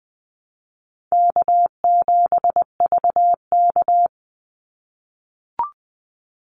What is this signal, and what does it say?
Morse code, heard as a keyed tone, about 20 wpm, 700 Hz. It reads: K7VK
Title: NA SOTA Activator Callsigns 20wpm